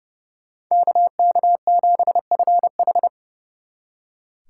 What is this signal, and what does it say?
Morse code, heard as a keyed tone, about 35 wpm, 700 Hz. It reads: KK7FH
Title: Speed Racing - USA Call Signs 30wpm